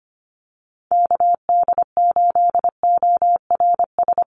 Morse code, heard as a keyed tone, about 25 wpm, 700 Hz. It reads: KD8ORS
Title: USA Call Signs Repeated 2x 25wpm